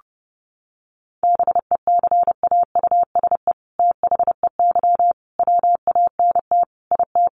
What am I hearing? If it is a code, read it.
Morse code, heard as a keyed tone, about 30 wpm, 700 Hz. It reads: BECAUSE THEY WANT IT
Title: Sentences from Top 800 Words 30wpm